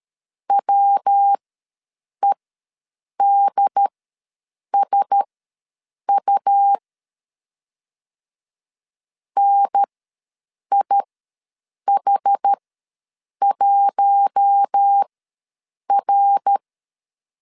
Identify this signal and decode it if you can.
Morse code, heard as a keyed tone, about 13 wpm, spaced as for 7 wpm, 800 Hz. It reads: WEDSU NIH1R
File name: LEZIONI / 5 LBVC / 5 L ws 12wpm 2958